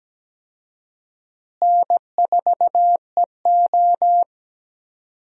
Morse code, heard as a keyed tone, about 17 wpm, 700 Hz. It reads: N4EO
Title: USA Call Signs - 4 Characters - Repeated 2x - 17wpm